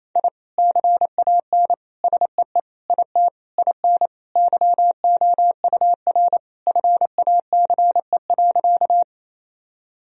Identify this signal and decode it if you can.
Morse code, heard as a keyed tone, about 28 wpm, 700 Hz. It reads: I CAN SEE IT IN YOUR FACE.
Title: Sentences from Top 1000 Words - Review Part 2 - 28wpm